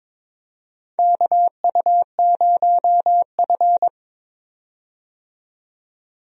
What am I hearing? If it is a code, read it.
Morse code, heard as a keyed tone, about 22 wpm, 700 Hz. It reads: KU0F
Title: USA Call Signs - 4 Characters - 22wpm